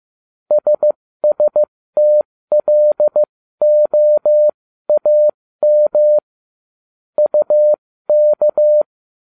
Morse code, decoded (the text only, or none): SSTLOAM UK